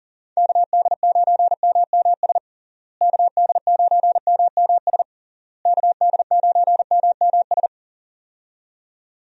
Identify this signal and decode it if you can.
Morse code, heard as a keyed tone, about 37 wpm, 700 Hz. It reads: KD9MMS KD9MMS KD9MMS
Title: USA Call Signs - 6 Characters - Repeated 3x - 40wpm